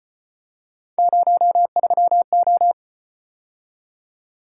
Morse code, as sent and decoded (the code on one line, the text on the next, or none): ----- ...-- ---
03O